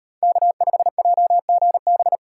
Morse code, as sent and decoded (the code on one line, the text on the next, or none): -.- ..... .--- --. -...
K5JGB